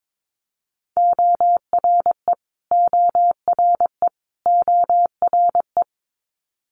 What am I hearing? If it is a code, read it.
Morse code, heard as a keyed tone, about 22 wpm, 700 Hz. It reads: ORE ORE ORE
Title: Multi-letter Phonemes Repeated 3x 22wpm